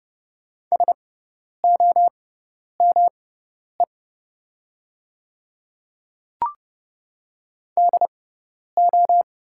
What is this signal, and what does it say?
Morse code, heard as a keyed tone, about 30 wpm, spaced as for 10 wpm, 690 Hz. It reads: SOME DO
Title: 6x Character Spacing - Top 200 Words - 30wpm